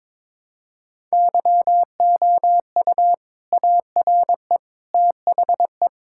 Morse code, heard as a keyed tone, about 22 wpm, 700 Hz. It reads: YOU ARE THE
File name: practice / 22wpm / Sentences with 2-Word Phrases - 22wpm